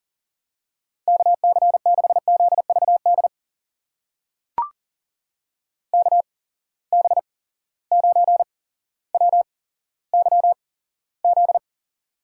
Morse code, heard as a keyed tone, about 40 wpm, spaced as for 17 wpm, 700 Hz. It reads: KC6ZVD KB9WYZ
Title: USA Call Signs - 6 Characters - 8x Character Spacing - 40wpm